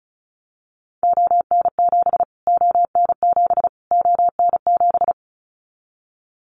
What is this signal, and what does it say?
Morse code, heard as a keyed tone, about 34 wpm, 700 Hz. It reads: ON7 ON7 ON7
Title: Callsign Prefixes for Top 25 DXCC - Repeated 3x - 35wpm